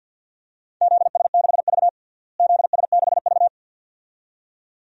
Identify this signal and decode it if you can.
Morse code, heard as a keyed tone, about 53 wpm, 700 Hz. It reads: ZS6V ZS6V
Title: DXCC-100 Call Signs - 4 Characters - Repeated 2x - 50wpm